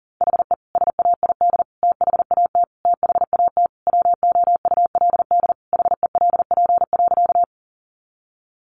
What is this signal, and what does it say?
Morse code, heard as a keyed tone, about 40 wpm, 700 Hz. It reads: HE SAID THAT THAT WOULD HELP.
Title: Sentences from Top 200 Words 40wpm